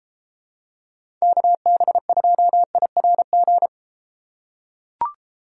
Morse code, heard as a keyed tone, about 31 wpm, 700 Hz. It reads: KB2IRG